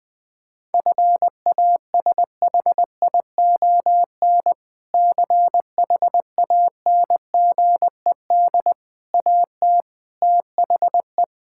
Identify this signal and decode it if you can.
Morse code, heard as a keyed tone, about 21 wpm, 700 Hz. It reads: FASHION CHANGED AT THE
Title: Sentences with 5-Word Phrases - 20wpm